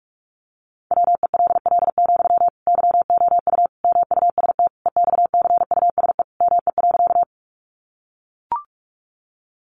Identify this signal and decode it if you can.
Morse code, heard as a keyed tone, about 44 wpm, 700 Hz. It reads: WELL, YOU MUST EXCUSE ME.
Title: Sentences from Top 700 Words 45wpm